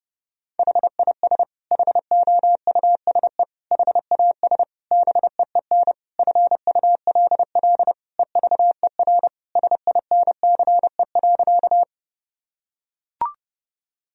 Morse code, heard as a keyed tone, about 32 wpm, 700 Hz. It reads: HIS HOUSE HAS BEEN FULL EVER SINCE.
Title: Sentences from Top 1000 Words - Review Part 3 - 30wpm